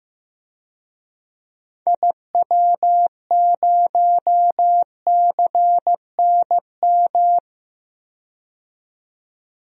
Morse code, heard as a keyed tone, about 15 wpm, 700 Hz. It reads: IW0CNM
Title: DXCC-100 Call Signs - 6 Characters - 15wpm